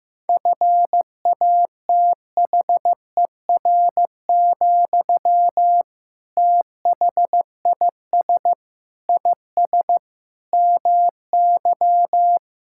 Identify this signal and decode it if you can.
Morse code, heard as a keyed tone, about 15 wpm, 700 Hz. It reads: FATHER, THIS IS MY
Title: Sentences from Top 1000 Words - Review Part 6 - 15wpm